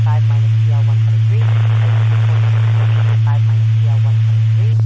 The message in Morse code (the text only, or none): none